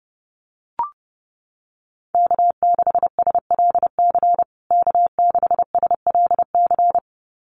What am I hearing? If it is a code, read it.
Morse code, heard as a keyed tone, about 30 wpm, 700 Hz. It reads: K6SLC K6SLC
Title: USA Call Signs Repeated 2x 30wpm